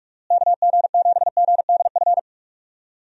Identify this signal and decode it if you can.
Morse code, heard as a keyed tone, about 46 wpm, 700 Hz. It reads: KG7GDF
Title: USA Call Signs - 6 Characters - Repeated 3x - 45wpm